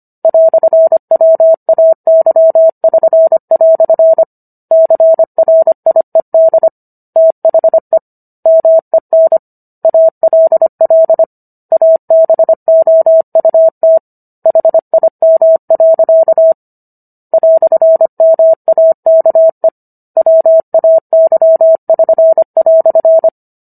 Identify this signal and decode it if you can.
Morse code, heard as a keyed tone, about 25 wpm, 650 Hz. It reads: "WAY<SN>" CRIED THE MEN ALL ABOUT HIM. "MAKE WAY<SN>"